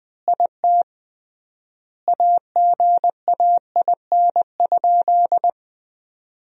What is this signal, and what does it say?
Morse code, heard as a keyed tone, about 20 wpm, 700 Hz. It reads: IT AGAIN?
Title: 3x Word-Spacing - Sentences from Top 200 Words - 20wpm